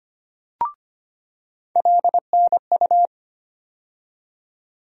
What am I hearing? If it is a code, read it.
Morse code, heard as a keyed tone, about 25 wpm, 700 Hz. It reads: LNU
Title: USA Call Sign Postfixes - 3 Letters 25wpm